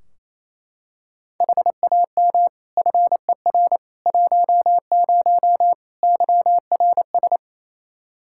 Morse code, heard as a keyed tone, about 28 wpm, 700 Hz. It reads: HAM FER 10 YRS
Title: K1USN SST QSO Practice 28wpm